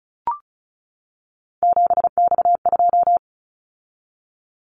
Morse code, heard as a keyed tone, about 36 wpm, 700 Hz. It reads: 7X2
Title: Callsign Prefixes for Top 100 DXCC - 35wpm